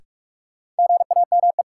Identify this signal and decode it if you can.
Morse code, heard as a keyed tone, about 42 wpm, 700 Hz. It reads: GAME
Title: Sentences from Top 500 Words - New Word Drill - 45wpm